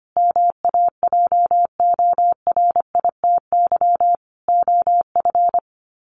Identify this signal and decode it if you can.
Morse code, heard as a keyed tone, about 25 wpm, 700 Hz. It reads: MAJORITY OF
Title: Sets of 4 Words 25wpm